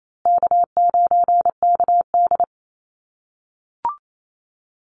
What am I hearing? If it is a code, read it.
Morse code, heard as a keyed tone, about 28 wpm, 700 Hz. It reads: K9KD
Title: USA Call Signs Repeated 3x 28wpm